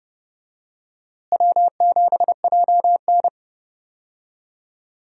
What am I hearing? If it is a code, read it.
Morse code, heard as a keyed tone, about 31 wpm, 700 Hz. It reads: W7JN
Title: USA Call Signs - 4 Characters - Repeated 3x - 30wpm